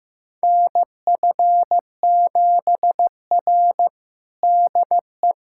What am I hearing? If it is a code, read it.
Morse code, heard as a keyed tone, about 15 wpm, 700 Hz. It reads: NF7R DE
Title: USA Callsign DE Callsign - 4 Characters - 15wpm